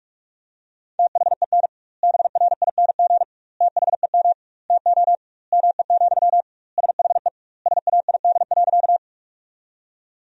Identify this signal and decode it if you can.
Morse code, heard as a keyed tone, about 42 wpm, 700 Hz. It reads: THEN BRING THEM TO ME, SHE SAID.
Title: 2x Word-Spacing - Sentences from Top 500 Words - 45wpm